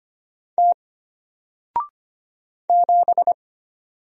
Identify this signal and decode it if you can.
Morse code, heard as a keyed tone, about 25 wpm, 700 Hz. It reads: T 7
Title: Single Letter-Number 25wpm